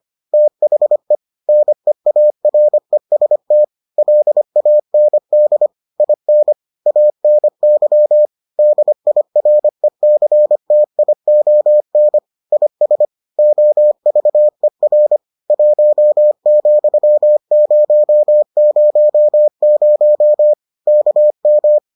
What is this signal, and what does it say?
Morse code, heard as a keyed tone, about 25 wpm, 600 Hz. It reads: THE NEAREST LAND IN ANY DIRECTION IS OVER 1,000 KM